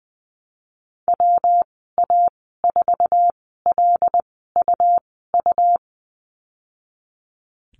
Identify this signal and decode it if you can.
Morse code, heard as a keyed tone, about 20 wpm, spaced as for 15 wpm, 700 Hz. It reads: WA4LUU
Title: 128 Lesson 21g - Number 6 - Call Signs 20wpm